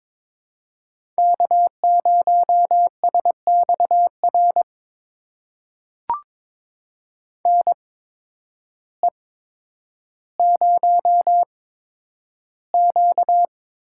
Morse code, heard as a keyed tone, about 22 wpm, spaced as for 10 wpm, 700 Hz. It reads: K0SXR NE0Q
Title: USA Call Signs - 8x Character Spacing - 22wpm